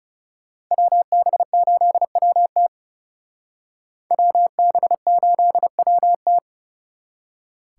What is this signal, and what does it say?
Morse code, heard as a keyed tone, about 36 wpm, 700 Hz. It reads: WB8WT WB8WT
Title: Speed Racing - USA 2x2 Call Signs 30wpm